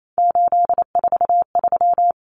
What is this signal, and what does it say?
Morse code, heard as a keyed tone, about 28 wpm, 700 Hz. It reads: 843
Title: Common Cut Numbers - 3-Digit 28wpm